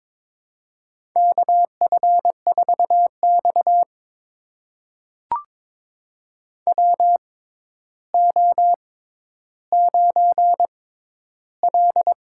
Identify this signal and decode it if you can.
Morse code, heard as a keyed tone, about 23 wpm, spaced as for 11 wpm, 700 Hz. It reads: KF4X WO9L
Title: USA Call Signs - 4 Characters - 6x Character Spacing - 22wpm